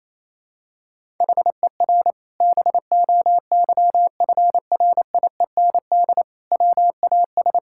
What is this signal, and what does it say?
Morse code, heard as a keyed tone, about 28 wpm, 700 Hz. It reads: HER BOYFRIEND WAS
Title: Sentences from Top 1000 Words - Review Part 5 - 28wpm